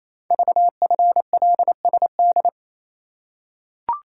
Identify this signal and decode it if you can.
Morse code, heard as a keyed tone, about 27 wpm, 700 Hz. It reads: VFLSD
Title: Code Groups - 5 Letters - 28wpm